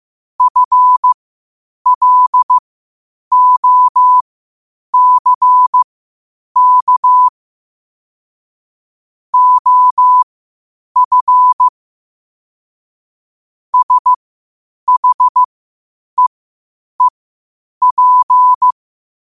Morse code, heard as a keyed tone, about 15 wpm, spaced as for 8 wpm, 1000 Hz. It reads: FLOCK OF SHEEP
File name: qotdfiles / 8wpm